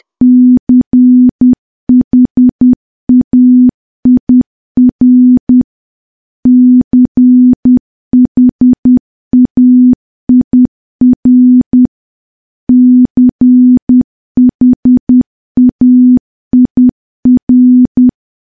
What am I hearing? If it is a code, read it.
Morse code, heard as a keyed tone, about 10 wpm, 250 Hz. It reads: CHAIR CHAIR CHAIR
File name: morse / morse-training-10WPM